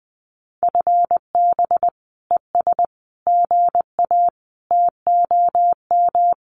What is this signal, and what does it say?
Morse code, heard as a keyed tone, about 20 wpm, 700 Hz. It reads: FB ES GA TOM
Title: Common QSO Phrases Encore 20wpm